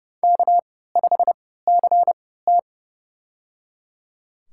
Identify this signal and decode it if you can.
Morse code, heard as a keyed tone, about 30 wpm, spaced as for 17 wpm, 700 Hz. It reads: K5CT
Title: USA Call Signs - 4 Characters - 3x Character Spacing - 30wpm